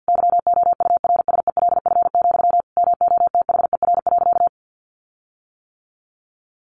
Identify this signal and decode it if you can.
Morse code, heard as a keyed tone, about 50 wpm, 700 Hz. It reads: YOURSELF, MOTHER.
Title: Sentences from Top 1300 Words 50wpm